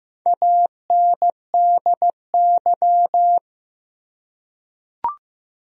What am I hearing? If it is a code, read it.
Morse code, heard as a keyed tone, about 15 wpm, 700 Hz. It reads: ANDY
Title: USA Common Names - Max 4 Letters - 15wpm